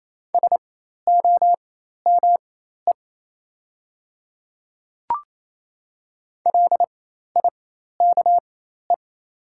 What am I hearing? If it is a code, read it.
Morse code, heard as a keyed tone, about 28 wpm, spaced as for 13 wpm, 690 Hz. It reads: SOME LIKE
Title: 4x Character Spacing - Top 300 Words - 28wpm